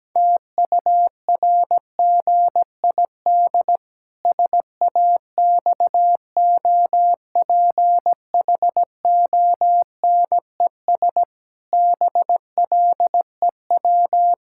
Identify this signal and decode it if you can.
Morse code, heard as a keyed tone, about 17 wpm, 700 Hz. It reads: TURGID SAXOPHONES BLEW
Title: Pangram Challenge v2 - 17wpm